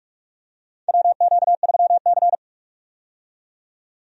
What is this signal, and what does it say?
Morse code, heard as a keyed tone, about 43 wpm, 700 Hz. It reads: WQ3C
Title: USA Call Signs - 4 Characters - 45wpm